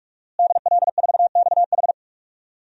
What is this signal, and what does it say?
Morse code, heard as a keyed tone, about 46 wpm, 700 Hz. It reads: DL4XH
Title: DXCC-100 Call Signs - 5 Characters - Repeated 3x - 45wpm